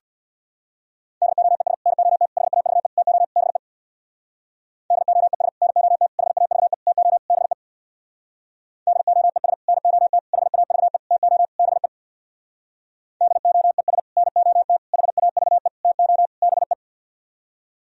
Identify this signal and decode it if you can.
Morse code, heard as a keyed tone, about 62 wpm, 700 Hz. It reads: DOES NOT HAVE TO BE DOES NOT HAVE TO BE DOES NOT HAVE TO BE DOES NOT HAVE TO BE
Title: Speed-Racing - Sets of 5 Words 50wpm